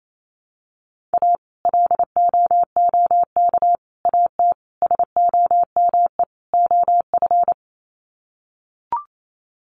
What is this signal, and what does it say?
Morse code, heard as a keyed tone, about 28 wpm, 700 Hz. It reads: A LOOK AT SOME OF
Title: Sets of 5 Words Encore2 28wpm